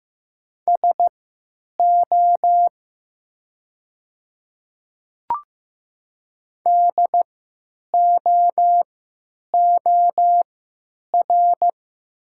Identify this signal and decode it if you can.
Morse code, heard as a keyed tone, about 15 wpm, spaced as for 9 wpm, 700 Hz. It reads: SO DOOR
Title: 3x Character Spacing - Top 300 Words - 15wpm